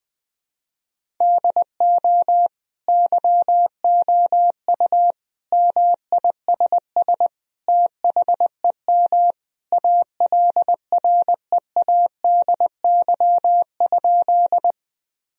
Morse code, heard as a keyed tone, about 20 wpm, 700 Hz. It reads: DO YOU MISS THEM ALREADY?
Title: Sentences from Top 1000 Words - Review Part 3 - 20wpm